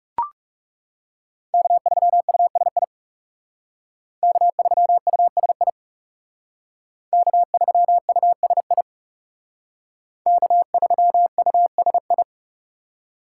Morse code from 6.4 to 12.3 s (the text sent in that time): K3USI K3USI